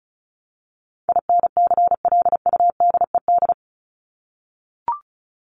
Morse code, heard as a keyed tone, about 34 wpm, 700 Hz. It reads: INCLUDED
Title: ED Suffix - Top 100 Words with Base Words 35wpm